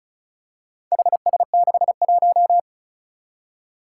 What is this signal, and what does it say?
Morse code, heard as a keyed tone, about 36 wpm, 700 Hz. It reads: HS61
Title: Four Challenging Letter-Number Combinations 35wpm